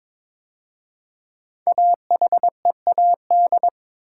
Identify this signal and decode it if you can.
Morse code, heard as a keyed tone, about 23 wpm, 700 Hz. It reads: AHEAD
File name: 5 Letter Words - First 3 Letters Given - 22wpm